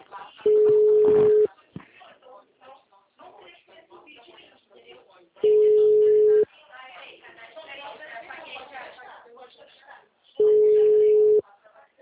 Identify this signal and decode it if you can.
Ringback tone (Europe, 425 Hz), with speech.